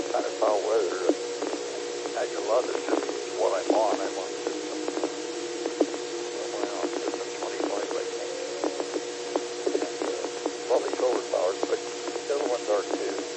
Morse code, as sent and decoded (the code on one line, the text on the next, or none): none